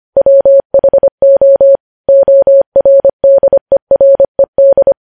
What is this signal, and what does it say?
Morse code, heard as a keyed tone, about 25 wpm, 550 Hz. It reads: WHO ORDERED
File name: podcast / News headlines-1x-25x25